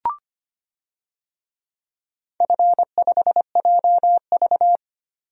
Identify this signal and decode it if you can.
Morse code, heard as a keyed tone, about 25 wpm, 700 Hz. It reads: F5JV
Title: DXCC-100 Call Signs - 4 Characters - 25wpm